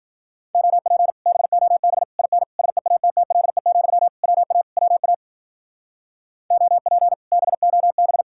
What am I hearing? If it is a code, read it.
Morse code, heard as a keyed tone, about 51 wpm, 700 Hz. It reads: OP BOB IN SEATTLE, WA WA OP BOB